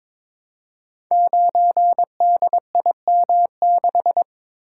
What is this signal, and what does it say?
Morse code, heard as a keyed tone, about 23 wpm, 700 Hz. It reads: 9DIM6
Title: Code Groups - 5 Letter:Numbers - 22wpm